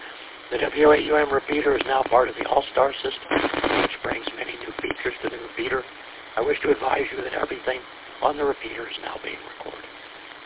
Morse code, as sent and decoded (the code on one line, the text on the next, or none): none